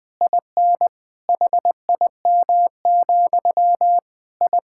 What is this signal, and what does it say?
Morse code, heard as a keyed tone, about 20 wpm, 700 Hz. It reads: IN HIM, I
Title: Sentences from Top 1000 Words - Review Part 6 - 20wpm